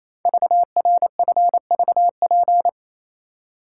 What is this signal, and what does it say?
Morse code, heard as a keyed tone, about 27 wpm, 700 Hz. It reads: VRFVP
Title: Code Groups - 5 Letters - 28wpm